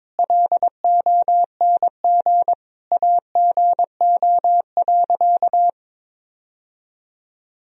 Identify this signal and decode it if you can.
Morse code, heard as a keyed tone, about 22 wpm, 700 Hz. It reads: LONG AGO.
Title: Sentences from Top 1000 Words - Review Part 3 - 22wpm